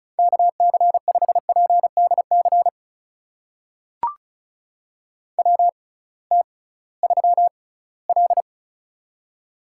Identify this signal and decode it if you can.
Morse code, heard as a keyed tone, about 36 wpm, spaced as for 20 wpm, 700 Hz. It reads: KC5PDC WT3L
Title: USA Call Signs - 6x Character Spacing - 35wpm